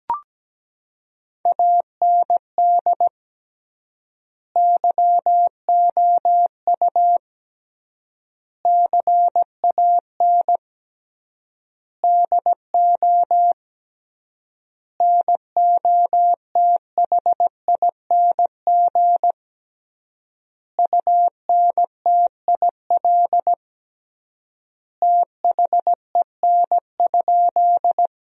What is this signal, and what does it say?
Morse code, heard as a keyed tone, about 17 wpm, 700 Hz. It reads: AND YOU CAN DO NOTHING UNTIL THEN?
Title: 3x Word-Spacing - Sentences from Top 500 Words - 17wpm